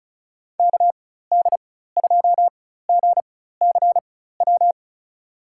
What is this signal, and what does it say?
Morse code, heard as a keyed tone, about 34 wpm, spaced as for 16 wpm, 690 Hz. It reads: KD2GCW
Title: USA Call Signs - 6 Characters - 4x Character Spacing - 35wpm